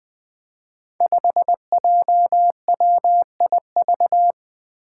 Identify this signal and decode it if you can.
Morse code, heard as a keyed tone, about 20 wpm, 700 Hz. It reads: 5JWIV
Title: Five Challenging Letter-Number Combinations 20wpm